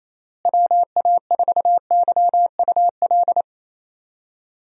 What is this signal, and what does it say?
Morse code, heard as a keyed tone, about 28 wpm, 700 Hz. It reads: WA4YUL
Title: USA Call Signs - 6 Characters - Repeated 3x - 28wpm